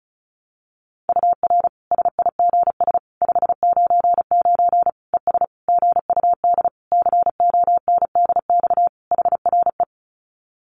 Morse code, heard as a keyed tone, about 34 wpm, 700 Hz. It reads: UR SIGS 599 ES GUD CONDX HRE
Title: Common QSO Phrases 35wpm